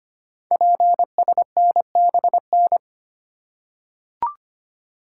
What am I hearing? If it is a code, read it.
Morse code, heard as a keyed tone, about 25 wpm, 700 Hz. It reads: PSNBN
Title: Code Groups - 5 Letters - 25wpm